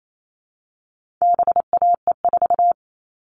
Morse code, heard as a keyed tone, about 28 wpm, 700 Hz. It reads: BAE4
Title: All Cut Numbers - 4-Digit 28wpm